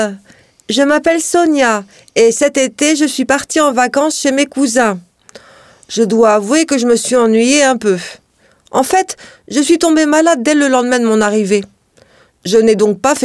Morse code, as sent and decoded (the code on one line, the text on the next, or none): none